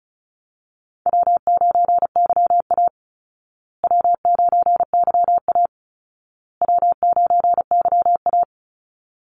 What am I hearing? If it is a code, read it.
Morse code, heard as a keyed tone, about 34 wpm, 700 Hz. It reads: W9YA W9YA W9YA
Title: Call Signs Repeated 2x - 35wpm